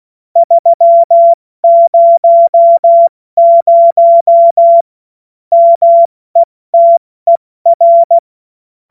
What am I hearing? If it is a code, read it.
Morse code, heard as a keyed tone, about 16 wpm, spaced as for 14 wpm, 680 Hz. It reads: 300 METER
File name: Morsekurs / Witze / Witz 30. Dezember 2025 70 F80